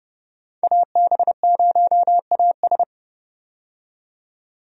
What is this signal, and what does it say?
Morse code, heard as a keyed tone, about 29 wpm, 700 Hz. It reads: AB0AS
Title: USA Call Signs - 5 Characters - 30wpm